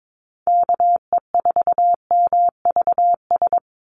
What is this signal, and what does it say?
Morse code, heard as a keyed tone, about 22 wpm, 700 Hz. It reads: KE4MVS
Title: USA Call Signs 22wpm